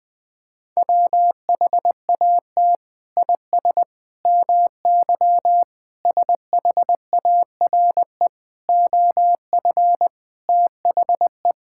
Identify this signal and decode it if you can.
Morse code, heard as a keyed tone, about 21 wpm, 700 Hz. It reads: WHAT IS MY SHARE OF THE
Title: Sentences from Top 1000 Words - Review Part 6 - 20wpm